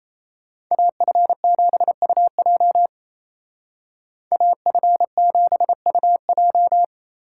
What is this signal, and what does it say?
Morse code, heard as a keyed tone, about 33 wpm, 700 Hz. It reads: AF7UJ AF7UJ
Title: Speed Racing - USA 2x2 Call Signs 28wpm